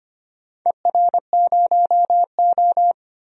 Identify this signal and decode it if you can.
Morse code, heard as a keyed tone, about 25 wpm, 700 Hz. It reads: ER0O
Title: DXCC-100 Call Signs - 4 Characters - 25wpm